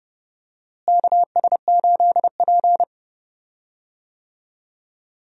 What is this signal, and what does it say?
Morse code, heard as a keyed tone, about 30 wpm, 700 Hz. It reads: KS8P
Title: USA Call Signs - 4 Characters - Repeated 3x - 30wpm